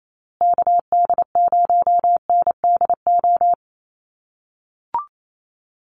Morse code, heard as a keyed tone, about 28 wpm, 700 Hz. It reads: KD0NDO
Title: USA Call Signs Repeated 3x 28wpm